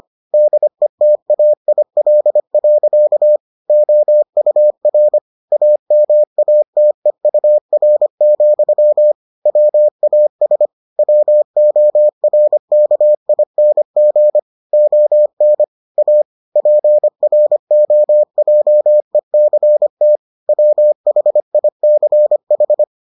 Morse code, heard as a keyed tone, about 25 wpm, 600 Hz. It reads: DETAIL. OUR AMATEUR, WAS WORKING ON A PROJECT WHICH